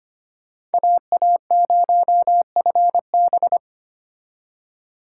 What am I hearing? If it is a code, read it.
Morse code, heard as a keyed tone, about 31 wpm, 700 Hz. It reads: AA0FB